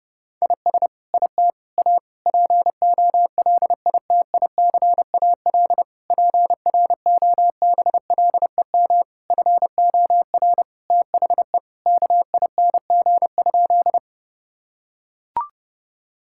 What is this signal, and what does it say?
Morse code, heard as a keyed tone, about 30 wpm, 700 Hz. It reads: IS IT A POLITICAL PROBLEM FOR THE KING?
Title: Sentences from Top 1000 Words - Review Part 5 - 30wpm